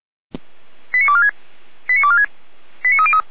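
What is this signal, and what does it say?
Five-tone selective call (ZVEI), heard as quick runs of five tones, tones of about 70 ms. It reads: 89246 89246 89303
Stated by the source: Five-tone ZVEI selective call received on an SDR.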